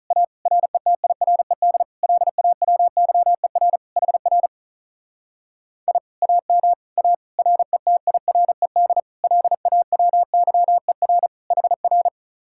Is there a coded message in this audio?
Yes: Morse code from 0.1 to 12.1 s, heard as a keyed tone, about 45 wpm, 700 Hz. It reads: A RETIRED LAWYER HR I AM A RETIRED LAWYER HR